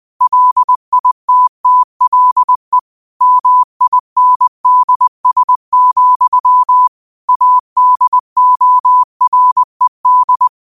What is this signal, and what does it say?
Morse code, heard as a keyed tone, about 20 wpm, 1000 Hz. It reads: LITTLE MINDS, ADORED
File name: qotdfiles / 20wpm